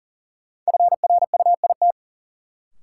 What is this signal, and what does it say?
Morse code, heard as a keyed tone, about 40 wpm, 700 Hz. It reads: FRUIT
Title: Sentences from Top 1300 Words - New Word Drill Alt - 40wpm